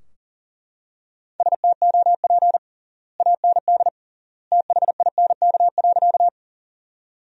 Morse code, heard as a keyed tone, about 40 wpm, 700 Hz. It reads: STOP AND THINK.
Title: 3x Word-Spacing - Sentences from Top 300 Words - 40wpm